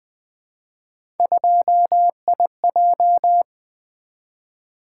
Morse code, heard as a keyed tone, about 20 wpm, 700 Hz. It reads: 2IJ
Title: Three Letter-Number Combinations - 20wpm